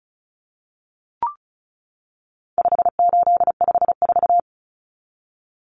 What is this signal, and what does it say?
Morse code, heard as a keyed tone, about 34 wpm, 700 Hz. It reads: 5854
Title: Common Cut Numbers - 4-Digit 35wpm